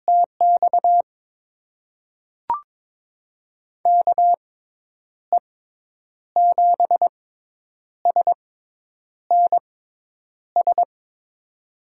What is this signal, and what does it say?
Morse code, heard as a keyed tone, about 22 wpm, spaced as for 9 wpm, 700 Hz. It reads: TX KE7SNS
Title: USA Call Signs - 6 Characters - 6x Character Spacing - 22wpm